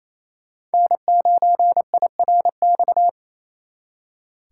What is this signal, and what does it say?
Morse code, heard as a keyed tone, about 35 wpm, 700 Hz. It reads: N9IRX